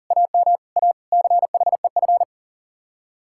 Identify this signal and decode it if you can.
Morse code, heard as a keyed tone, about 48 wpm, 700 Hz. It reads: AM A CHEF
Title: Speed-Racing Common QSO Phrases - 40wpm